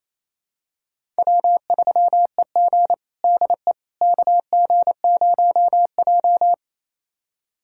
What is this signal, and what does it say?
Morse code, heard as a keyed tone, about 28 wpm, 700 Hz. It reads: W3EG DE KG0J
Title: USA Callsign DE Callsign - 4 Characters - 28wpm